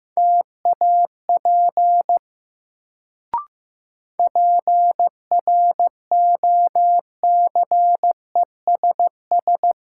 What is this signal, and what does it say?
Morse code, heard as a keyed tone, about 15 wpm, 700 Hz. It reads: TAP PROCESS
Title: Sentences from Top 1400 Words - New Word Drill Alt - 15wpm